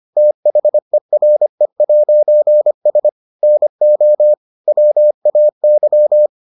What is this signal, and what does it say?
Morse code, heard as a keyed tone, about 25 wpm, 600 Hz. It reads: THERE'S NO WAY